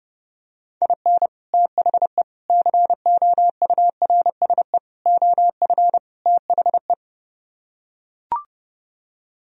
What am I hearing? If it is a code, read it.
Morse code, heard as a keyed tone, about 29 wpm, 700 Hz. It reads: IN THE COURSE OF THE